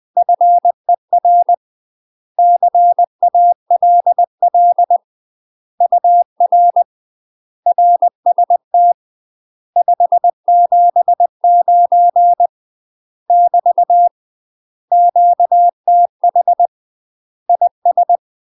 Morse code, heard as a keyed tone, about 20 wpm, 700 Hz. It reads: FER CALL UR RST 579 <BT> QTH IS
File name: UK MASTER QSOs TRAINING / 20WPM / UK MASTER QSOs 0035